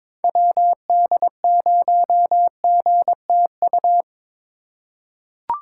Morse code, heard as a keyed tone, about 22 wpm, 700 Hz. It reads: WD0GTU
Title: USA Call Signs - 6 Characters - 3x Character Spacing - 22wpm